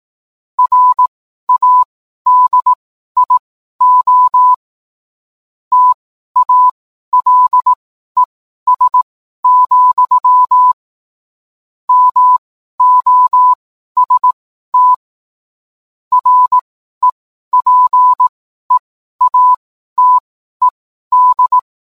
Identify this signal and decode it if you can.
Morse code, heard as a keyed tone, about 18 wpm, spaced as for 12 wpm, 1000 Hz. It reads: RADIO TALES, MOST REPEATED